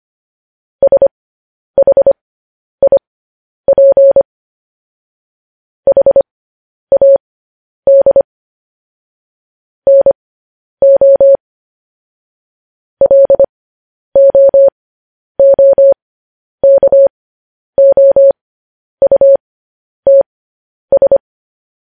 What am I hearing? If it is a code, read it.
Morse code, heard as a keyed tone, about 25 wpm, spaced as for 10 wpm, 550 Hz. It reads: SHIP HAD NO LOOKOUTS
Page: News Headlines in Morse Code at 10 WPM